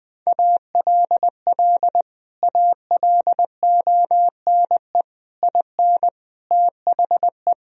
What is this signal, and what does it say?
Morse code, heard as a keyed tone, about 20 wpm, 700 Hz. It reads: ALL ALONE IN THE
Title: Sentences from Top 1000 Words - Review Part 2 - 20wpm